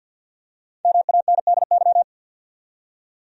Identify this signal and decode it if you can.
Morse code, heard as a keyed tone, about 52 wpm, 700 Hz. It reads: MANDY
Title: USA Common Names - 50wpm